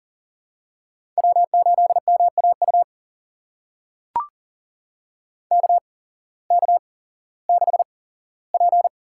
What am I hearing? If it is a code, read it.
Morse code, heard as a keyed tone, about 40 wpm, spaced as for 19 wpm, 700 Hz. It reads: W8MAU KK6P
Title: USA Call Signs - 8x Character Spacing - 40wpm